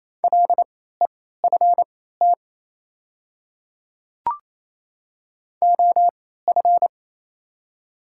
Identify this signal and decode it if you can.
Morse code, heard as a keyed tone, about 28 wpm, spaced as for 16 wpm, 700 Hz. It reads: LEFT OF
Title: 3x Character Spacing - Top 300 Words - 28wpm